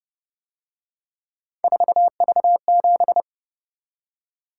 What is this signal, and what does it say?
Morse code, heard as a keyed tone, about 29 wpm, 700 Hz. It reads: 4V7
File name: Speed-Racing - Three Letter-Number Combinations - 25wpm